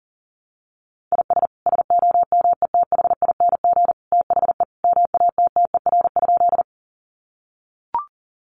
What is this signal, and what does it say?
Morse code, heard as a keyed tone, about 40 wpm, 700 Hz. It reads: IS SOMETHING THE MATTER?
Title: Sentences from Top 700 Words 40wpm